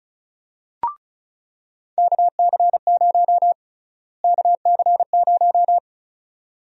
Morse code, heard as a keyed tone, about 36 wpm, 700 Hz. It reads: KC0 KC0
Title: USA Call Sign Prefixes - Repeated 2x - 35wpm